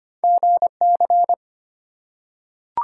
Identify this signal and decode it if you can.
Morse code, heard as a keyed tone, about 25 wpm, 700 Hz. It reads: GC
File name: Two Letter-Number Combinations - 25wpm